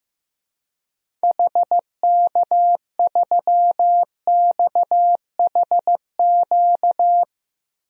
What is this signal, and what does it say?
Morse code, heard as a keyed tone, about 15 wpm, 700 Hz. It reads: HK3XHQ
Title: DXCC-100 Call Signs - 6 Characters - 15wpm